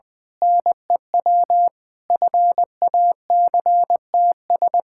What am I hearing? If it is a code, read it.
Morse code, heard as a keyed tone, about 20 wpm, 700 Hz. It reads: NEW FACTS
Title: Sentences with 5-Word Phrases - Spoken Beginning - 20wpm